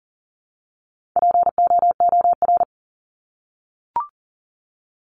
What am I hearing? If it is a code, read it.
Morse code, heard as a keyed tone, about 40 wpm, 700 Hz. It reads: POOR
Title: Top 751-1000 Words 40wpm